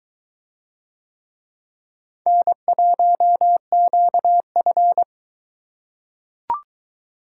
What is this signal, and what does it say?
Morse code, heard as a keyed tone, about 21 wpm, 700 Hz. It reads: N1QF